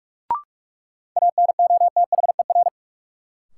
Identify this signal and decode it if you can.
Morse code, heard as a keyed tone, about 41 wpm, 700 Hz. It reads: ANOTHER
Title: Sentences from Top 600 Words - New Word Drill - 45wpm